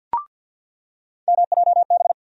Morse code, heard as a keyed tone, about 48 wpm, 700 Hz. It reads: MJB
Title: USA Call Sign Postfixes - 3 Letters 50wpm